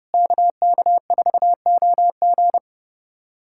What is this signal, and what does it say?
Morse code, heard as a keyed tone, about 30 wpm, 700 Hz. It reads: KK4OG
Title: USA Callsign DE Callsign - 5 Characters - 30wpm